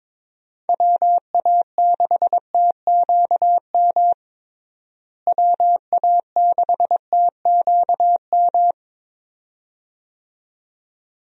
Morse code, heard as a keyed tone, about 22 wpm, 700 Hz. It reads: WA6TQM WA6TQM
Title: USA Call Signs - 6 Characters - Repeated 2x - 22wpm